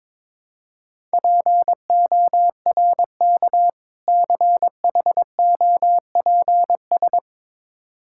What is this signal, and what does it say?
Morse code, heard as a keyed tone, about 22 wpm, 690 Hz. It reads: PORK CHOPS
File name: Difficult Words with Hints - 22wpm